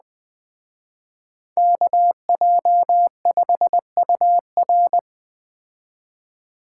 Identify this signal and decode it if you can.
Morse code, heard as a keyed tone, about 20 wpm, 700 Hz. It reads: KJ5UR